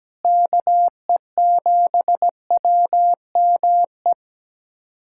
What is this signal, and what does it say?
Morse code, heard as a keyed tone, about 17 wpm, 700 Hz. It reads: KE7WME
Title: USA Callsign DE Callsign - 6 Characters - 17wpm